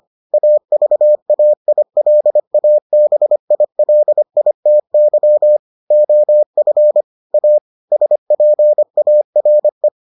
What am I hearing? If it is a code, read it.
Morse code, heard as a keyed tone, about 25 wpm, 600 Hz. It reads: AVAILABILITY OF A SPARE